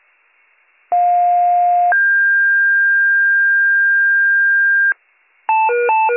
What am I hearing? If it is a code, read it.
Two-tone page, heard as a one-second tone, then a three-second tone, about 710 then 1600 Hz. It is heard from 0.9 to 4.9 s.